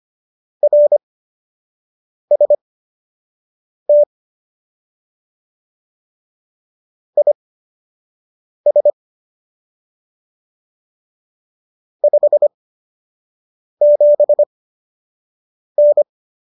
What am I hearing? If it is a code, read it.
Morse code, heard as a keyed tone, about 25 wpm, spaced as for 6 wpm, 590 Hz. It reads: RST IS 57N